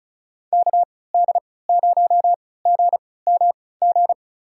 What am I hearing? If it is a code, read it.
Morse code, heard as a keyed tone, about 34 wpm, spaced as for 20 wpm, 690 Hz. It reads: KD0GMG
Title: USA Call Signs - 6 Characters - 3x Character Spacing - 35wpm